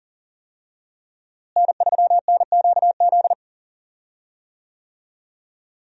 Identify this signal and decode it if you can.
Morse code, heard as a keyed tone, about 38 wpm, 700 Hz. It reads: N3NQZ
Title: USA Call Signs - 5 Characters - Repeated 3x - 40wpm